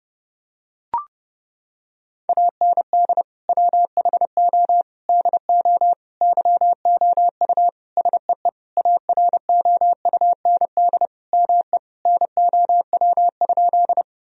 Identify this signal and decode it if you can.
Morse code, heard as a keyed tone, about 30 wpm, 700 Hz. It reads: AND WHO DO YOU SEE AROUND ME NOW?
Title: Sentences from Top 1000 Words - Review Part 3 - 30wpm